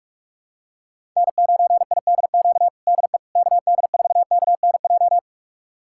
Morse code, heard as a keyed tone, about 46 wpm, 700 Hz. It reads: N9IDQ DE KD4KNJ
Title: USA Callsign DE Callsign 45wpm